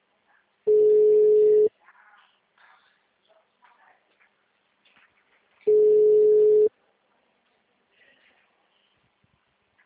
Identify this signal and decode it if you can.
Ringback tone (Europe, 425 Hz).
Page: Atibaia - SP